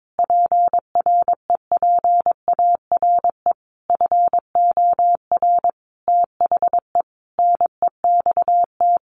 Morse code, heard as a keyed tone, about 22 wpm, 700 Hz. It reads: PREPARE FOR THE NEXT
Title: Sentences from Top 1200 Words 22wpm